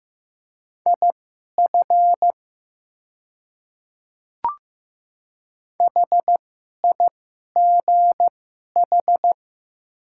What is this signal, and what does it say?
Morse code, heard as a keyed tone, about 15 wpm, spaced as for 11 wpm, 700 Hz. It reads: IF HIGH
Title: 2x Character Spacing - Top 200 Words - 15wpm